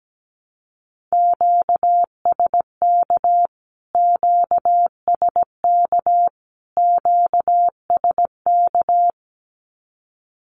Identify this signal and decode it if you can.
Morse code, heard as a keyed tone, about 17 wpm, 700 Hz. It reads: QSK QSK QSK
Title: Q-Codes Repeated 3x 17wpm